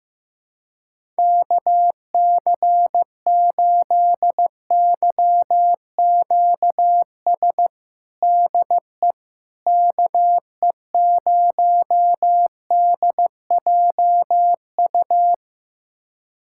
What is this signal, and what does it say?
Morse code, heard as a keyed tone, about 16 wpm, 700 Hz. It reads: KC8YQS DE KE0DJU
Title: USA Callsign DE Callsign 15wpm